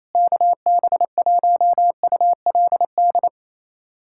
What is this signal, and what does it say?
Morse code, heard as a keyed tone, about 28 wpm, 700 Hz. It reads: KB1ULD
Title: USA Callsign DE Callsign - 6 Characters - 28wpm